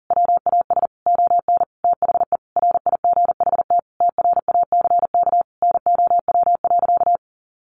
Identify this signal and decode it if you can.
Morse code, heard as a keyed tone, about 40 wpm, 700 Hz. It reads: WAS ON THE RIGHT TRACK NOW.
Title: Sentences from Top 1100 Words 40wpm